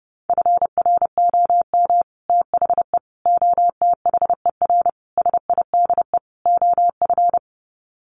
Morse code, heard as a keyed tone, about 30 wpm, 700 Hz. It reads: FROM THE OTHER SIDE OF
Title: Sets of 5 Words Encore 30wpm